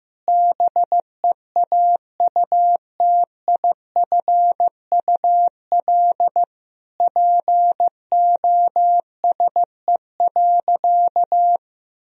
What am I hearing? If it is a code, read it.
Morse code, heard as a keyed tone, about 14 wpm, 700 Hz. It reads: BEAUTIFUL POSE.
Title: Sentences from Top 1000 Words - Review Part 2 - 15wpm